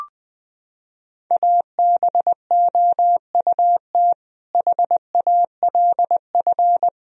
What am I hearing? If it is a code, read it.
Morse code, heard as a keyed tone, about 21 wpm, 700 Hz. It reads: ABOUT HALF
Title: Sentences from Top 1000 Words - Review Part 3 - 20wpm